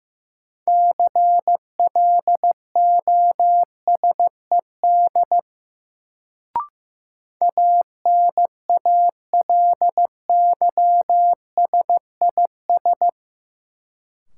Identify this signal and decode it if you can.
Morse code, heard as a keyed tone, about 15 wpm, 700 Hz. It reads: CLOSED ANALYSIS
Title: Sentences from Top 1400 Words - New Word Drill Alt - 15wpm